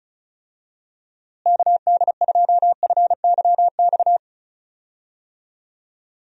Morse code, heard as a keyed tone, about 36 wpm, 700 Hz. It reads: KD2FYX
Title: USA Call Signs - 6 Characters - Repeated 2x - 35wpm